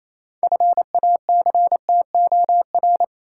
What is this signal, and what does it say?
Morse code, heard as a keyed tone, about 28 wpm, 700 Hz. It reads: FACTOR
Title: Six Letter Words - 28wpm